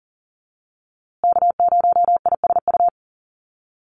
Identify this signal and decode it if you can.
Morse code, heard as a keyed tone, about 40 wpm, 700 Hz. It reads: K0ISU
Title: USA Call Signs Repeated 2x 40wpm